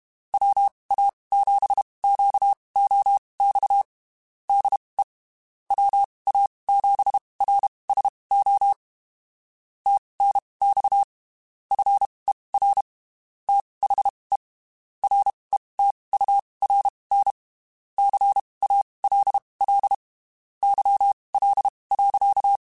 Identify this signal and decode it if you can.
Morse code, heard as a keyed tone, about 32 wpm, spaced as for 23 wpm, 800 Hz. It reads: WA7QOX DE WA7RSO TNX FER THE RETURN CALL YL.
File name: (spaced) / QSOs Spaced-mp3s / WA7QOX 2 @30 sp